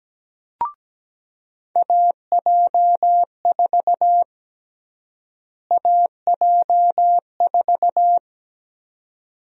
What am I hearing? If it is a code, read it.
Morse code, heard as a keyed tone, about 17 wpm, 700 Hz. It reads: AJ4 AJ4
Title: USA Call Sign Prefixes - Repeated 2x - 17wpm